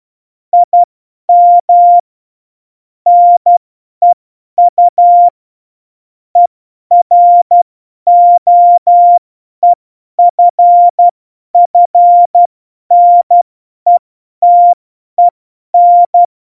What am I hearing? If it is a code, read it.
Morse code, heard as a keyed tone, about 12 wpm, spaced as for 10 wpm, 700 Hz. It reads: IM NEU EROEFFNETEN
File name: Morsekurs / Witze / Witz 20. November 2018, 50 F70